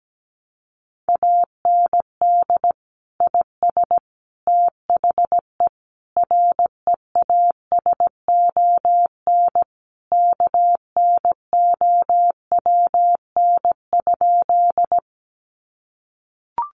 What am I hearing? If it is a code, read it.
Morse code, heard as a keyed tone, about 17 wpm, 700 Hz. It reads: AND IS THE REASON KNOWN?
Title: Sentences from Top 700 Words 17wpm